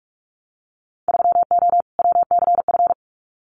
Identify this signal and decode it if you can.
Morse code, heard as a keyed tone, about 46 wpm, 700 Hz. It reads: 3O WCF
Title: Winter Field Day Exchanges Encore 45wpm